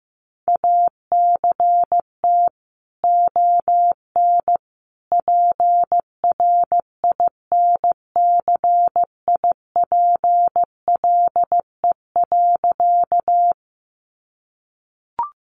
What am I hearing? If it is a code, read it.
Morse code, heard as a keyed tone, about 15 wpm, 700 Hz. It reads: ACT ON PRINCIPLE.
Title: Sentences from Top 1400 Words 15wpm